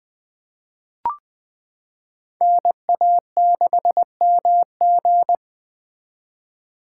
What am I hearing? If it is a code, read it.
Morse code, heard as a keyed tone, about 20 wpm, 700 Hz. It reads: NA6MG
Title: 2023 SOTA Chaser Call Signs - 20wpm